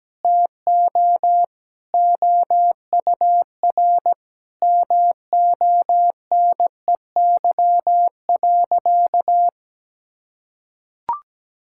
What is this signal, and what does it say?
Morse code, heard as a keyed tone, about 17 wpm, 700 Hz. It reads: TO OUR MONEY.
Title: Sentences from Top 1600 Words 17wpm